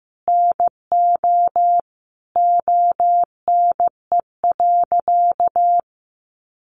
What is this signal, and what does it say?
Morse code, heard as a keyed tone, about 15 wpm, 700 Hz. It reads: NO ONE.
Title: Additional Easy Phrases 15wpm